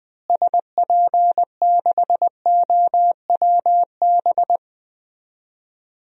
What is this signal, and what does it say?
Morse code, heard as a keyed tone, about 21 wpm, 700 Hz. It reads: SP6OWB
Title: DXCC-100 Call Signs - 6 Characters - Repeated 2x - 20wpm